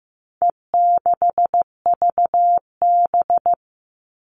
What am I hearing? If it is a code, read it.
Morse code, heard as a keyed tone, about 15 wpm, 700 Hz. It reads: E6VB
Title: All Cut Numbers - 4-Digit 15wpm